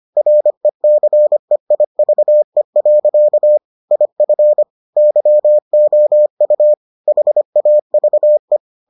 Morse code, heard as a keyed tone, about 25 wpm, 600 Hz. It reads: RECEIVE. IF YOU HAVE